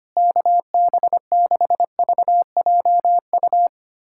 Morse code, heard as a keyed tone, about 25 wpm, 700 Hz. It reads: KB6VJU
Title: USA Callsign DE Callsign - 6 Characters - 25wpm